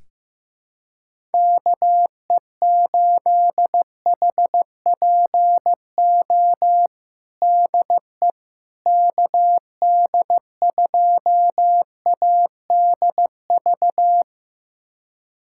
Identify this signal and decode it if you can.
Morse code, heard as a keyed tone, about 15 wpm, 700 Hz. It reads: KE8HPO DE KD2ADV
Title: USA Callsign DE Callsign - 6 Characters - 15wpm